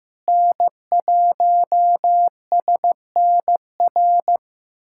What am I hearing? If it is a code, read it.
Morse code, heard as a keyed tone, about 16 wpm, 700 Hz. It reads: N1SNR
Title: USA Callsign DE Callsign - 5 Characters - 15wpm